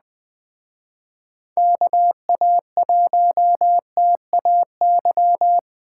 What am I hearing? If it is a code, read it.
Morse code, heard as a keyed tone, about 20 wpm, 700 Hz. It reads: KA1TAY
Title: Speed Racing - USA 2x3 Call Signs 17wpm